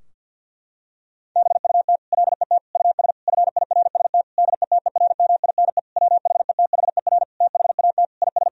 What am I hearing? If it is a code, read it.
Morse code, heard as a keyed tone, about 50 wpm, 700 Hz. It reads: BUT LET US FIRST DETERMINE WHETHER THAT IS
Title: Sentences from Top 1000 Words - Review Part 4 - 50wpm